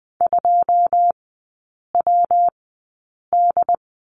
Morse code, heard as a keyed tone, about 20 wpm, spaced as for 8 wpm, 700 Hz. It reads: 2WD
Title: 127 Lesson 21e - Number 6 - Three Characters 20wpm